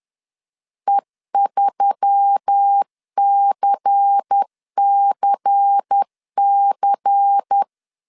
Morse code, heard as a keyed tone, about 10 wpm, 800 Hz. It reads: E3CCC
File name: CW / DET / 1 6 1500C S B 10wpm 4137